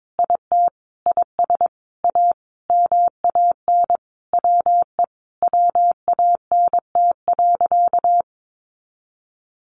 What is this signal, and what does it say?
Morse code, heard as a keyed tone, about 22 wpm, 700 Hz. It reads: IT IS A MAN WE WANT.
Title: Sentences from Top 200 Words 22wpm